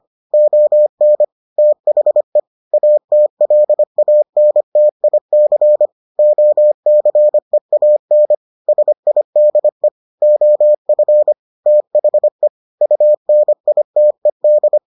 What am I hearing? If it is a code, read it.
Morse code, heard as a keyed tone, about 25 wpm, 600 Hz. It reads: ON THE ATLANTIC OCEAN SIDE OF THE UNITED